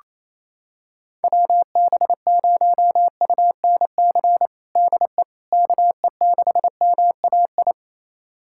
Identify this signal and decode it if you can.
Morse code, heard as a keyed tone, about 28 wpm, 700 Hz. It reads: WB0UNC DE KE6MAI
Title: USA Callsign DE Callsign 28wpm